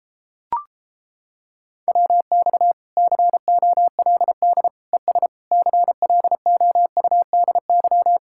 Morse code, heard as a keyed tone, about 29 wpm, 700 Hz. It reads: WX COLD ES CLOUDY